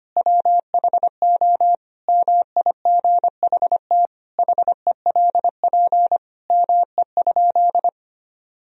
Morse code, heard as a keyed tone, about 25 wpm, 700 Hz. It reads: WHO MIGHT HELP ME?
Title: Sentences from Top 1000 Words - Review Part 3 - 25wpm